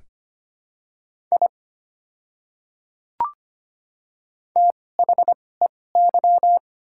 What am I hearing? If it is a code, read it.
Morse code, heard as a keyed tone, about 25 wpm, spaced as for 18 wpm, 700 Hz. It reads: I THEY
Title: 2x Character Spacing - Top 100 Words - 25wpm